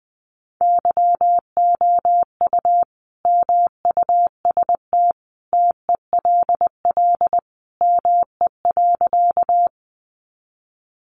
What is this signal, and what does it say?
Morse code, heard as a keyed tone, about 20 wpm, 700 Hz. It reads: YOU MUST TELL ME.
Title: Sentences from Top 200 Words 20wpm